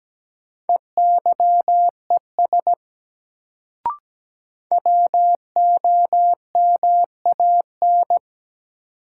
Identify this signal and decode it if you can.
Morse code, heard as a keyed tone, about 17 wpm, 700 Hz. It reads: EYES WOMAN
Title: Sets of 5 Words - New Word Drill - 17wpm